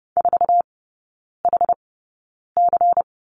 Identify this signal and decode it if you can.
Morse code, heard as a keyed tone, about 30 wpm, spaced as for 9 wpm, 700 Hz. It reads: 4HC
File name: Lesson 16e - Letter c - Three Characters 30wpm